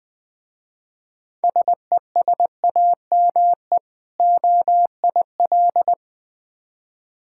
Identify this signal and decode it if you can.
Morse code, heard as a keyed tone, about 20 wpm, 700 Hz. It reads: SESAME OIL
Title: Difficult Words with Hints - 20wpm